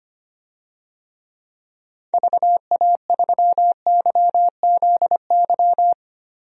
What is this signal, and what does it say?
Morse code, heard as a keyed tone, about 25 wpm, 700 Hz. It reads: VA3YZY
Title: DXCC-100 Call Signs - 6 Characters - Repeated 2x - 25wpm